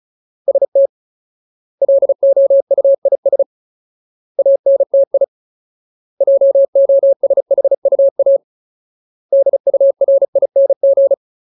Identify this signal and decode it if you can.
Morse code, heard as a keyed tone, about 36 wpm, 550 Hz. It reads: ST LOUIS ANTI JOSHUA DURING
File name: Ultimate Word List / 35WPM Ultimate